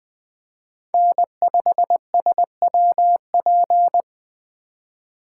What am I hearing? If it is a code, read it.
Morse code, heard as a keyed tone, about 20 wpm, 700 Hz. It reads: N5SWP
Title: Speed Racing - USA 1x3 Call Signs 17wpm